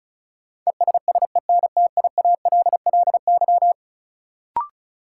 Morse code, heard as a keyed tone, about 36 wpm, 700 Hz. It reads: ESSENTIALLY
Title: Sentences from Top 1300 Words - New Word Drill - 35wpm